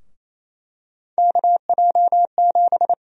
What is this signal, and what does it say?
Morse code, heard as a keyed tone, about 28 wpm, 700 Hz. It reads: KJ7
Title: USA Call Sign Prefixes - Repeated 2x - 28wpm